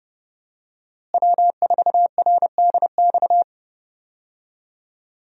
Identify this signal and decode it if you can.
Morse code, heard as a keyed tone, about 31 wpm, 700 Hz. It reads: W4RDX
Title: USA Call Signs - 5 Characters - Repeated 2x - 30wpm